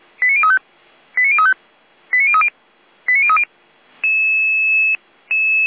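Five-tone selective call (ZVEI), heard as quick runs of five tones, tones of about 70 ms. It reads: 89035 89035 89030 89030
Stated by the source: Five-tone ZVEI selective call received by SDR.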